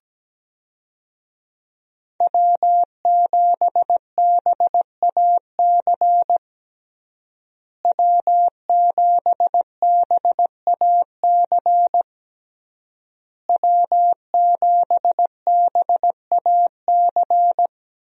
Morse code, heard as a keyed tone, about 17 wpm, 700 Hz. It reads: W7BAC W7BAC W7BAC
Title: USA Call Signs - 5 Characters - Repeated 3x - 17wpm